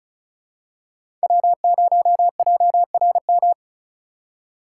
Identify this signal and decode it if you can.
Morse code, heard as a keyed tone, about 34 wpm, 700 Hz. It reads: W0JRM
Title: USA Call Signs - 4x Character Spacing - 35wpm